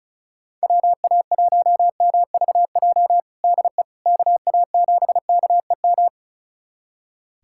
Morse code, heard as a keyed tone, about 34 wpm, 700 Hz. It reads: WA1MVJ DE KA7KEM
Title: USA Callsign DE Callsign - 6 Characters - 35wpm